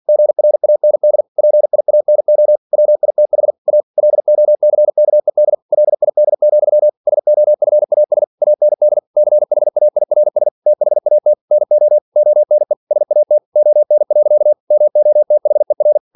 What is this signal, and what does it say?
Morse code, heard as a keyed tone, about 46 wpm, 600 Hz. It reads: GRAND PIANO WITH A LOCKED LID, SOFAS AND CHAIRS THAT NO ONE SAT ON. MOTHER
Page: Never 47wpm